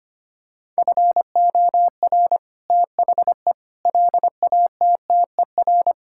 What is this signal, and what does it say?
Morse code, heard as a keyed tone, about 25 wpm, 700 Hz. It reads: FOR THE LATTER
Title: Sentences from Top 1600 Words 25wpm